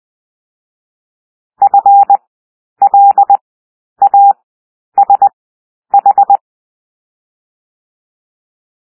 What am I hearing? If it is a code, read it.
Morse code, heard as a keyed tone, about 20 wpm, 800 Hz. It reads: F L A S H
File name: cwmp3 / W1 SOUND WORDS 20 wpm